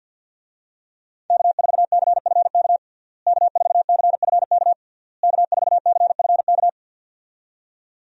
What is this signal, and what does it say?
Morse code, heard as a keyed tone, about 50 wpm, 700 Hz. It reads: K4CFK K4CFK K4CFK
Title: USA Call Signs - 5 Characters - Repeated 3x - 50wpm